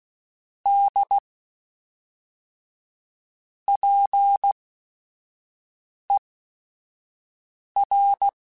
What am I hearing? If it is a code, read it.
Morse code, heard as a keyed tone, about 16 wpm, spaced as for 5 wpm, 800 Hz. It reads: D PER
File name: Technical Subjects / Technical Subjects-mp3s / Sw Pwr Sup @5